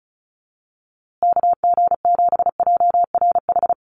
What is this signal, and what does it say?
Morse code, heard as a keyed tone, about 34 wpm, 700 Hz. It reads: KG7JRH
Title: USA Call Signs Repeated 2x 35wpm